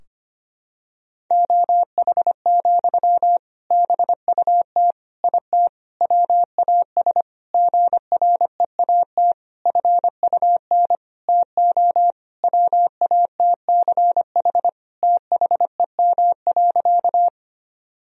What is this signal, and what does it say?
Morse code, heard as a keyed tone, about 25 wpm, 700 Hz. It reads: OH, BUT IT WAS GREAT FUN TO WATCH THEM.
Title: Sentences from Top 1000 Words - Review Part 4 - 25wpm